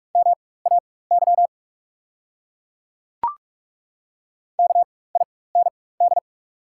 Morse code, heard as a keyed tone, about 42 wpm, spaced as for 21 wpm, 690 Hz. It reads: MAY KIND
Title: 4x Character Spacing - Top 200 Words - 45wpm